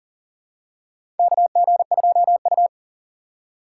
Morse code, heard as a keyed tone, about 40 wpm, 700 Hz. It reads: KG2U
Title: USA Call Signs - 4 Characters - 3x Character Spacing - 40wpm